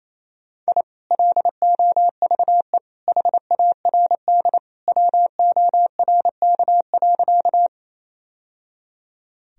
Morse code, heard as a keyed tone, about 28 wpm, 700 Hz. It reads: I LOVE HARD WORK.
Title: Sentences from Top 1000 Words - Review Part 1 - 28wpm